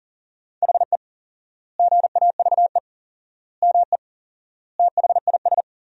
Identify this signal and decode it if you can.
Morse code, heard as a keyed tone, about 40 wpm, 700 Hz. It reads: HE GAVE ME THIS
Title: 4x Word-Spacing - Sentences from Top 500 Words - 40wpm